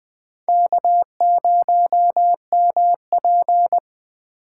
Morse code, heard as a keyed tone, about 20 wpm, 700 Hz. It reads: K0MP
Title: 2023 SOTA Chaser Call Signs - 20wpm